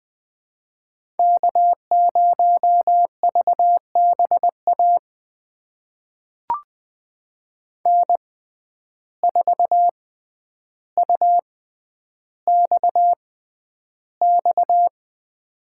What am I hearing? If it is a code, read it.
Morse code, heard as a keyed tone, about 20 wpm, spaced as for 10 wpm, 700 Hz. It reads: K0VBA N4UXX
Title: USA Call Signs - 5 Characters - 6x Character Spacing - 20wpm